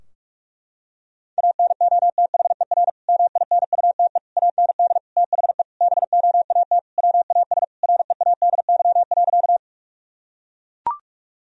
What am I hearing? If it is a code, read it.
Morse code, heard as a keyed tone, about 41 wpm, 700 Hz. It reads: ANOTHER MINUTE AND THE BOAT WAS READY.
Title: Sentences from Top 1000 Words - Review Part 3 - 45wpm